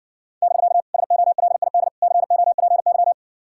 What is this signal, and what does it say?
Morse code, heard as a keyed tone, about 60 wpm, 700 Hz. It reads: <BK> SOLID COPY